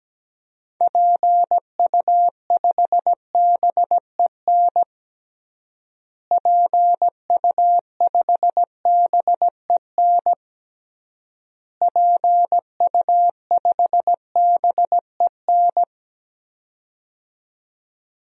Morse code, heard as a keyed tone, about 17 wpm, 700 Hz. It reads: PU5BEN PU5BEN PU5BEN
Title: DXCC-100 Call Signs - 6 Characters - Repeated 3x - 17wpm